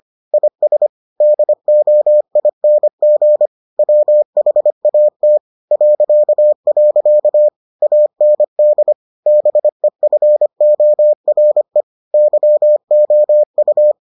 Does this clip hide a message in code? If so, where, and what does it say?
Yes: Morse code from 0.3 to 13.9 s, heard as a keyed tone, about 25 wpm, 600 Hz. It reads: IS DOING WHAT .. AND BEFORE YOU